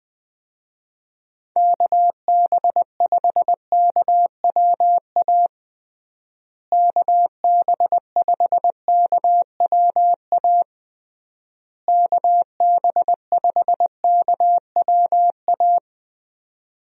Morse code, heard as a keyed tone, about 20 wpm, 700 Hz. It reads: KB5KWA KB5KWA KB5KWA
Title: USA Call Signs - 6 Characters - Repeated 3x - 20wpm